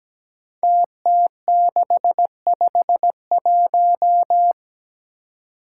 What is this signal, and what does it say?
Morse code, heard as a keyed tone, about 17 wpm, 700 Hz. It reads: TT651
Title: Five Challenging Letter-Number Combinations 17wpm